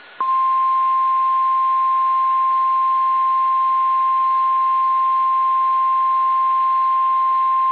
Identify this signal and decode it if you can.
Emergency Alert System (SAME) tones (the 1050 Hz attention tone). It is heard from 0.6 to 6.7 s.
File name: eas / KSDZKDJL / KSDZKDJL